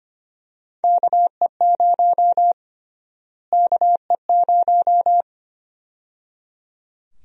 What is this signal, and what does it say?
Morse code, heard as a keyed tone, about 25 wpm, 700 Hz. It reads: KE0 KE0
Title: USA Call Sign Prefixes - Repeated 2x - 25wpm